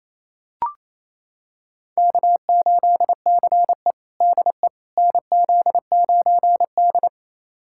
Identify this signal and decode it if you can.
Morse code, heard as a keyed tone, about 28 wpm, 700 Hz. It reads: K8CE DE NZ9D
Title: USA Callsign DE Callsign - 4 Characters - 28wpm